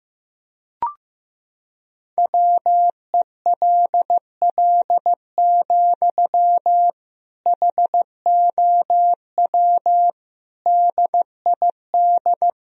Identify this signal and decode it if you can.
Morse code, heard as a keyed tone, about 15 wpm, 700 Hz. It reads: WELL, HOW DID
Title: Sentences from Top 1000 Words - Review Part 2 - 15wpm